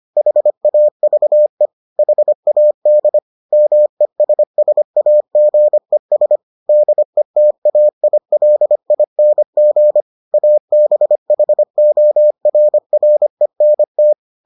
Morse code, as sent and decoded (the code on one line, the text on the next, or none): .... .- ...- . / .... .- -.. / -- . ... ... .- --. ... / -.. - .- .. .-.. .. -. --. / .- -... .... --- .-. .-. -. -
HAVE HAD MESSAGES DETAILING ABHORRENT